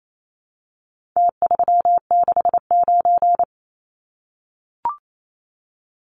Morse code, heard as a keyed tone, about 28 wpm, 700 Hz. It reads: T369
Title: Numbers - Contest Style - 4-Digit 28wpm